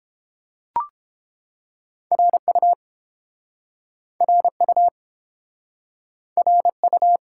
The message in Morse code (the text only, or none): RU RU RU